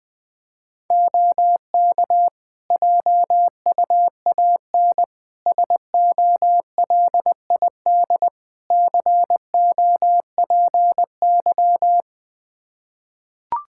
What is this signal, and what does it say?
Morse code, heard as a keyed tone, about 21 wpm, 700 Hz. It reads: OK JUAN SOLID COPY
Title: K1USN SST QSO Practice 20wpm